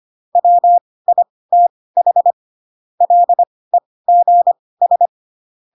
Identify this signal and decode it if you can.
Morse code, heard as a keyed tone, about 25 wpm, spaced as for 18 wpm, 700 Hz. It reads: WITH LEGS
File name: Pinocchio ch3 12 25 ik1hge